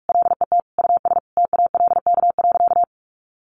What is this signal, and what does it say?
Morse code, heard as a keyed tone, about 44 wpm, 700 Hz. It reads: LET US TALK.
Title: Sentences from Top 1300 Words 45wpm